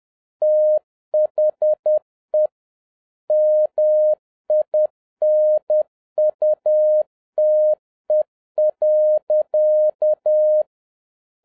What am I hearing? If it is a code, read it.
Morse code, heard as a keyed tone, about 10 wpm, 610 Hz. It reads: THE MINUTE.
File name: 2011-07-mp3 / n36 THE 3.11 2 10wpm